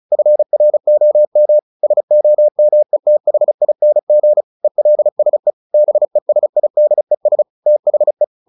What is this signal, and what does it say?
Morse code, heard as a keyed tone, about 35 wpm, 600 Hz. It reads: FROM SOMETHING ELSE BESIDES THE
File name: the old man and the sea 2 35wpm